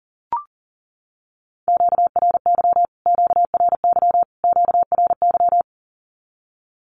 Morse code, heard as a keyed tone, about 40 wpm, 700 Hz. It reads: QRY QRY QRY
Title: Q-Codes Repeated 3x 40wpm